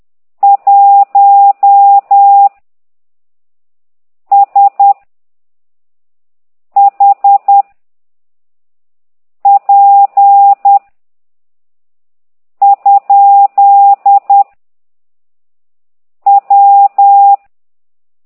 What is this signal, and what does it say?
Morse code, heard as a keyed tone, about 10 wpm, 800 Hz. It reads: 1 S H P ? W
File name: ct2idw / exerc15